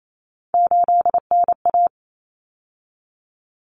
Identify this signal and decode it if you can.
Morse code, heard as a keyed tone, about 28 wpm, 700 Hz. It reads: 8NA
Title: Common Cut Numbers - 3-Digit 28wpm